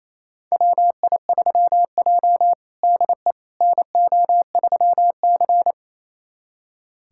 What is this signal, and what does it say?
Morse code, heard as a keyed tone, about 28 wpm, 690 Hz. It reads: WI3J DE NO3C
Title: USA Callsign DE Callsign - 4 Characters - 28wpm